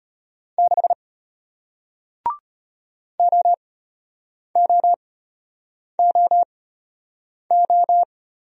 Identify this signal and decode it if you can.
Morse code, heard as a keyed tone, about 33 wpm, spaced as for 8 wpm, 700 Hz. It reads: 6 OOOO